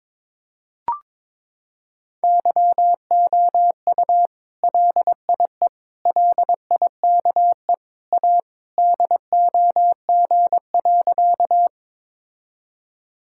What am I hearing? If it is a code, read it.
Morse code, heard as a keyed tone, about 22 wpm, 700 Hz. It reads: YOU LIE LIKE A DOG.
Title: Sentences from Top 1600 Words 22wpm